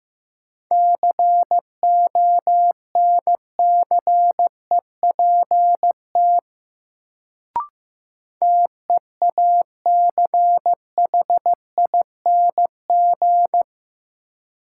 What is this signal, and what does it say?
Morse code, heard as a keyed tone, about 15 wpm, 700 Hz. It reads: CONCEPT TEACHING
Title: Sentences from Top 1100 Words - New Word Drill - 15wpm